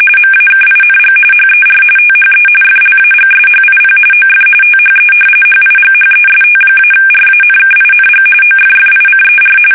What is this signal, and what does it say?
RTTY, about 45.45 baud. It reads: MAKING HER THE FLAGSHIP OF THE HIGHEST-RANKING OFFICER
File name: fsk850 / battleshipnj